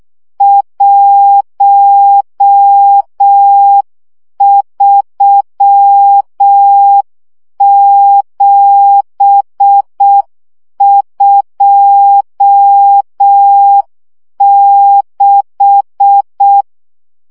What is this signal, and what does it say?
Morse code, heard as a keyed tone, about 6 wpm, 800 Hz. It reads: 13726
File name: morze vizsga 2011-2 5